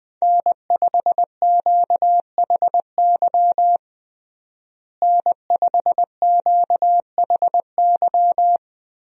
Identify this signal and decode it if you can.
Morse code, heard as a keyed tone, about 21 wpm, 700 Hz. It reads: N5QHY N5QHY
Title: USA Call Signs - 5 Characters - Repeated 3x - 20wpm